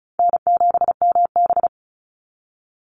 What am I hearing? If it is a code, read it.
Morse code, heard as a keyed tone, about 34 wpm, 700 Hz. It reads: N7MB
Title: Call Signs - 35wpm